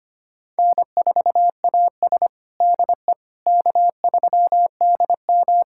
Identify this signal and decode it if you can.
Morse code, heard as a keyed tone, about 25 wpm, 700 Hz. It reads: N4AS DE K3DM
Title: USA Callsign DE Callsign - 4 Characters - 25wpm